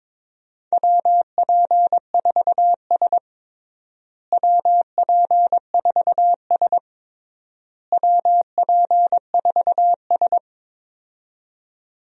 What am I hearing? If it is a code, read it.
Morse code, heard as a keyed tone, about 22 wpm, 700 Hz. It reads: WP4S WP4S WP4S
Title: USA Call Signs - 4 Characters - Repeated 3x - 22wpm